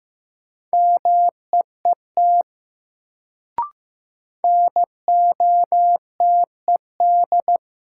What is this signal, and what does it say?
Morse code, heard as a keyed tone, about 15 wpm, 700 Hz. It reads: MEET NOTED
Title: Sets of 5 Words - New Word Drill - 15wpm